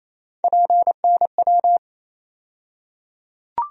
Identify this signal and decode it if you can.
Morse code, heard as a keyed tone, about 28 wpm, 700 Hz. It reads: PNW
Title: USA Call Sign Postfixes - 3 Letters 28wpm